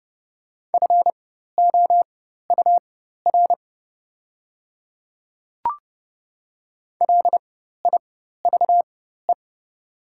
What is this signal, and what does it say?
Morse code, heard as a keyed tone, about 30 wpm, spaced as for 14 wpm, 690 Hz. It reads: FOUR LIVE
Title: 4x Character Spacing - Top 300 Words - 30wpm